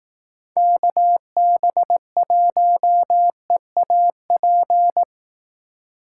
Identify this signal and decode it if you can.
Morse code, heard as a keyed tone, about 18 wpm, 700 Hz. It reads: KB1EAP
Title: Speed Racing - USA 2x3 Call Signs 15wpm